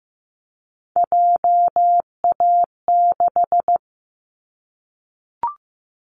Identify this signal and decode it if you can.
Morse code, heard as a keyed tone, about 15 wpm, 700 Hz. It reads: JA6
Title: Callsign Prefixes for Top 25 DXCC - 15wpm